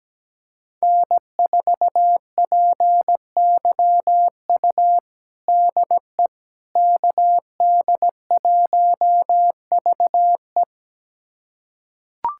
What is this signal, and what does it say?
Morse code, heard as a keyed tone, about 17 wpm, 700 Hz. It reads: N4PYU DE KD1VE
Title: USA Callsign DE Callsign - 5 Characters - 17wpm